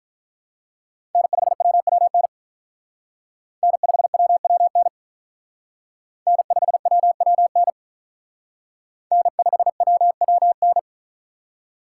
Morse code, heard as a keyed tone, about 44 wpm, 700 Hz. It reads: N5WWN N5WWN N5WWN N5WWN